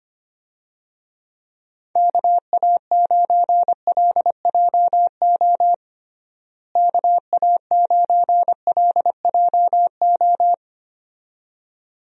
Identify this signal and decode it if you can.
Morse code, heard as a keyed tone, about 25 wpm, 700 Hz. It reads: KA9LJO KA9LJO
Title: USA Call Signs - 6 Characters - Repeated 2x - 25wpm